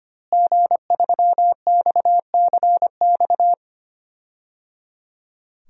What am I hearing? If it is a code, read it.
Morse code, heard as a keyed tone, about 25 wpm, 700 Hz. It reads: G3XCX
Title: DXCC-100 Call Signs - 5 Characters - Repeated 3x - 25wpm